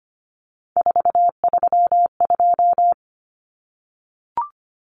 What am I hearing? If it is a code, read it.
Morse code, heard as a keyed tone, about 25 wpm, 700 Hz. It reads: 432
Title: Numbers - 3-Digit 25wpm